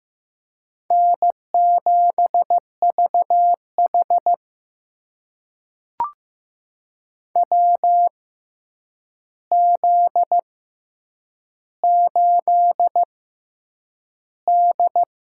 Morse code, heard as a keyed tone, about 15 wpm, spaced as for 8 wpm, 700 Hz. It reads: N7VH WZ8D
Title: USA Call Signs - 4 Characters - 6x Character Spacing - 15wpm